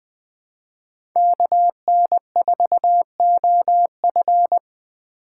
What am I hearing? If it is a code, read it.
Morse code, heard as a keyed tone, about 20 wpm, 700 Hz. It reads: KN4OF
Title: USA Call Signs - 5 Characters - 2x Character Spacing - 20wpm